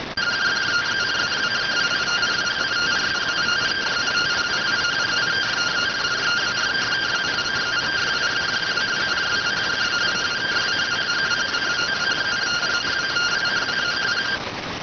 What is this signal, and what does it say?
Two-tone teleprinter signal, about 170 Hz shift.